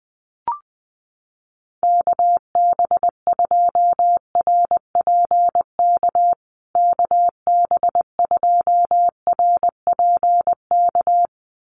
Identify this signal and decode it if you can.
Morse code, heard as a keyed tone, about 20 wpm, 700 Hz. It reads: KB2RPK KB2RPK
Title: USA Call Signs Repeated 2x 20wpm